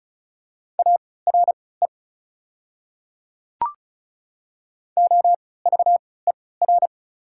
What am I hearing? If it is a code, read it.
Morse code, heard as a keyed tone, about 34 wpm, spaced as for 20 wpm, 700 Hz. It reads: ARE OVER
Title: 3x Character Spacing - Top 200 Words - 35wpm